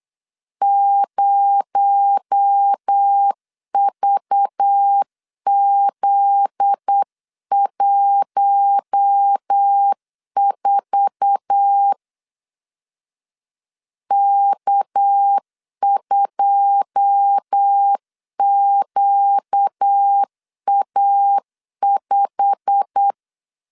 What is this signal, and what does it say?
Morse code, heard as a keyed tone, about 8 wpm, 800 Hz. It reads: 0VZ14 K2QA5
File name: CW / DET / 1 9 1500C S B 8wpm 5624